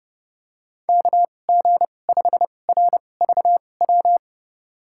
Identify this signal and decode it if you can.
Morse code, heard as a keyed tone, about 30 wpm, spaced as for 22 wpm, 700 Hz. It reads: KG5RVW
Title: USA Call Signs - 6 Characters - 2x Character Spacing - 30wpm